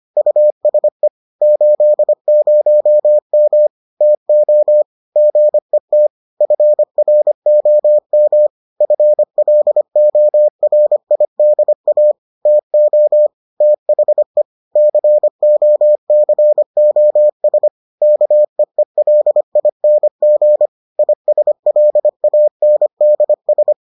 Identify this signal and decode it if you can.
Morse code, heard as a keyed tone, about 25 wpm, 600 Hz. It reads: USE 80M TO GET FROM FLORIDA TO THE COCOS KEELING ISLANDS